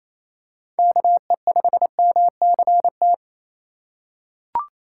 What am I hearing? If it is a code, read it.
Morse code, heard as a keyed tone, about 27 wpm, 700 Hz. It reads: KE5MCT
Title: USA Call Signs - 6 Characters - 8x Character Spacing - 28wpm